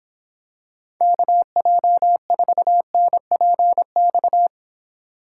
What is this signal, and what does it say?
Morse code, heard as a keyed tone, about 26 wpm, 700 Hz. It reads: KJ4NPX